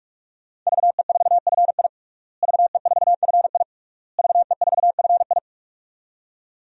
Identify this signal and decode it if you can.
Morse code, heard as a keyed tone, about 46 wpm, 700 Hz. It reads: VE4FI VE4FI VE4FI
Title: DXCC-100 Call Signs - 5 Characters - Repeated 3x - 45wpm